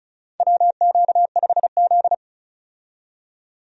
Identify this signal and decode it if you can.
Morse code, heard as a keyed tone, about 34 wpm, 700 Hz. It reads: WQ5Z
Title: USA Call Signs - 4 Characters - 35wpm